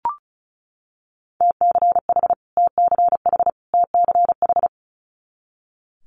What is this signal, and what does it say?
Morse code, heard as a keyed tone, about 36 wpm, 700 Hz. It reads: TCH TCH TCH
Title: Multi-letter Phonemes Repeated 3x 35wpm